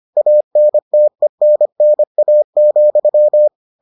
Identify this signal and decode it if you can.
Morse code, heard as a keyed tone, about 25 wpm, 600 Hz. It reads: ANTENNA,